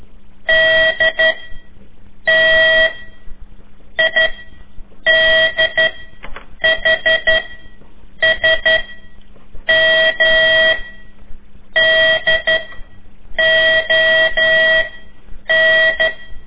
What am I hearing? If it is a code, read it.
Morse code, heard as a keyed tone, about 12 wpm, 1950 Hz. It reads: D T I D H S M D O N
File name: ik1weg2 / CORSI CW / lez4-3